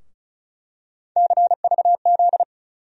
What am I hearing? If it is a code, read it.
Morse code, heard as a keyed tone, about 34 wpm, 700 Hz. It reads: CVZ
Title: USA Call Sign Postfixes - 3 Letters 35wpm